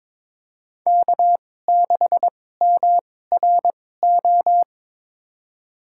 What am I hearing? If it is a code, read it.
Morse code, heard as a keyed tone, about 22 wpm, spaced as for 16 wpm, 700 Hz. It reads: K6MRO
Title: USA Call Signs - 5 Characters - 2x Character Spacing - 22wpm